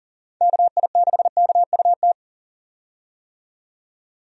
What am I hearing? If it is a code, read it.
Morse code, heard as a keyed tone, about 40 wpm, 700 Hz. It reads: KI6KUT
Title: USA Call Signs - 6 Characters - 40wpm